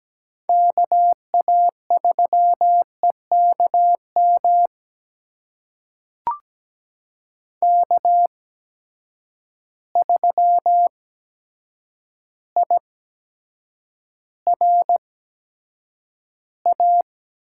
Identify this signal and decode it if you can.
Morse code, heard as a keyed tone, about 17 wpm, spaced as for 8 wpm, 700 Hz. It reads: KA3EKM K3IRA
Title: USA Call Signs - 8x Character Spacing - 17wpm